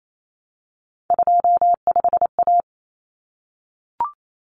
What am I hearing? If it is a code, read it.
Morse code, heard as a keyed tone, about 28 wpm, 700 Hz. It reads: 25A
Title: Common Cut Numbers - 3-Digit 28wpm